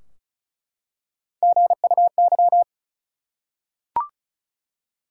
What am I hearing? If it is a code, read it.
Morse code, heard as a keyed tone, about 34 wpm, 700 Hz. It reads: GUY
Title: Three Letter Words - 35wpm